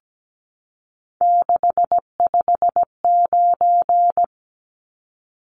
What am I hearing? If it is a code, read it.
Morse code, heard as a keyed tone, about 17 wpm, 700 Hz. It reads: 659
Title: Numbers - 3-Digit 17wpm